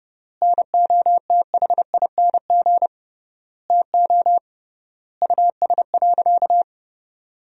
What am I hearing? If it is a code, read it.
Morse code, heard as a keyed tone, about 30 wpm, 700 Hz. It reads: NOTHING TO US.
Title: 3x Word-Spacing - Sentences from Top 500 Words - 30wpm